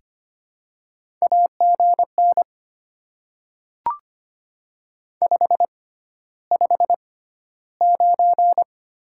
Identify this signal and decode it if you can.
Morse code, heard as a keyed tone, about 25 wpm, spaced as for 13 wpm, 700 Hz. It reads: AGN 559
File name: Top 90 QSO Elements - 6x Character Spacing - 25wpm